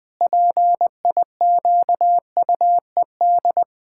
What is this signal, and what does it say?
Morse code, heard as a keyed tone, about 21 wpm, 700 Hz. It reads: PIQUED
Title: Pangram Challenge v2 - 20wpm